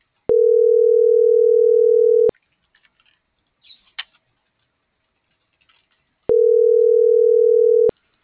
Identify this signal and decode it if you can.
Ringback tone (North America), 440 + 480 Hz.
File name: CENTRAL MORENA / out ruth.skyangel.24 - .1200